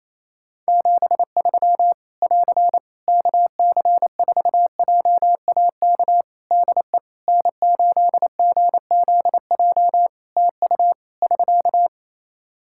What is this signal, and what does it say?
Morse code, heard as a keyed tone, about 27 wpm, 700 Hz. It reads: 73 <AR> KC4JAK DE N8GZJ TU <SK>
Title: Practice QSO - Exchange 4 - Ending - 28wpm